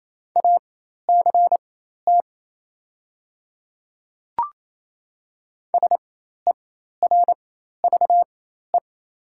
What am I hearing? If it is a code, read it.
Morse code, heard as a keyed tone, about 28 wpm, spaced as for 13 wpm, 700 Hz. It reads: ACT SERVE
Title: 4x Character Spacing - Top 500 Words - 28wpm